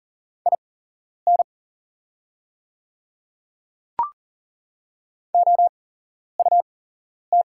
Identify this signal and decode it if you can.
Morse code, heard as a keyed tone, about 40 wpm, spaced as for 11 wpm, 690 Hz. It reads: IN OUT
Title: 8x Character Spacing - Top 100 Words - 40wpm